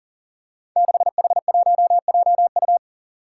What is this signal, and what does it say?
Morse code, heard as a keyed tone, about 40 wpm, 700 Hz. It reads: 6H1JU
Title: Five Challenging Letter-Number Combinations 40wpm